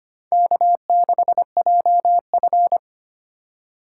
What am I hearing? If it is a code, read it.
Morse code, heard as a keyed tone, about 25 wpm, 700 Hz. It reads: K6JF
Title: USA Call Signs - 4 Characters - 4x Character Spacing - 25wpm